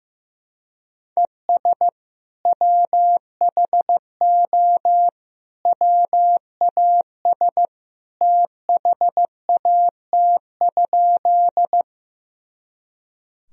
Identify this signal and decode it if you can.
Morse code, heard as a keyed tone, about 15 wpm, 700 Hz. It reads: ES WHO WAS THAT?
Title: Sentences from Top 100 Words and Abbreviations 15wpm